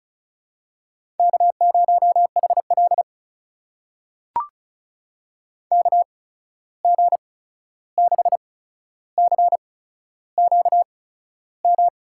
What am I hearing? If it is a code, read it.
Morse code, heard as a keyed tone, about 36 wpm, spaced as for 13 wpm, 700 Hz. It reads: K0HL KG6CQM
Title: USA Call Signs - 8x Character Spacing - 35wpm